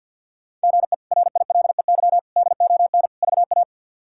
Morse code, heard as a keyed tone, about 52 wpm, 700 Hz. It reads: GE RILEY DON VA
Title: K1USN SST QSO Practice - 50wpm